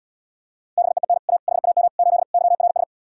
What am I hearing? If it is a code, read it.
Morse code, heard as a keyed tone, about 79 wpm, 700 Hz. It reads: BEEN A HAM 8 YRS